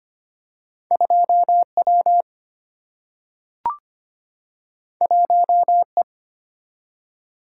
Morse code, heard as a keyed tone, about 25 wpm, 700 Hz. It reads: 2W 1E
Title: Single-Letter Single-Number 25wpm